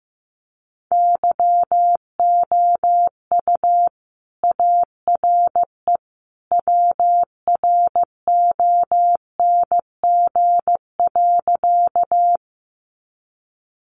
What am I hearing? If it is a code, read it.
Morse code, heard as a keyed tone, about 15 wpm, 700 Hz. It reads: YOU ARE WRONG.
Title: Sentences from Top 900 Words 15wpm